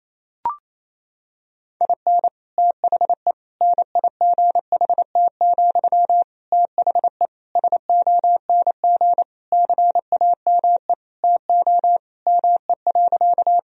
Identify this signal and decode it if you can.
Morse code, heard as a keyed tone, about 28 wpm, 700 Hz. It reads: IN THE NIGHT, THE SONG CAME TO ME.
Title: Sentences from Top 1000 Words - Review Part 2 - 28wpm